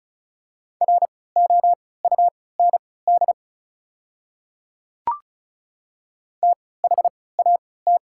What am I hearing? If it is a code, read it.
Morse code, heard as a keyed tone, about 36 wpm, spaced as for 20 wpm, 690 Hz. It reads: ROUND THAT
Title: 3x Character Spacing - Top 200 Words - 35wpm